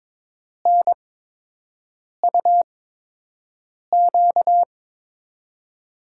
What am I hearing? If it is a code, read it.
Morse code, heard as a keyed tone, about 22 wpm, spaced as for 6 wpm, 700 Hz. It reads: NUQ
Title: USA Call Signs - 8x Character Spacing - 22wpm